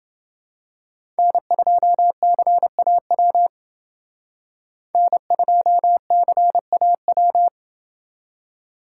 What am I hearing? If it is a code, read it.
Morse code, heard as a keyed tone, about 28 wpm, 700 Hz. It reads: N2CAW N2CAW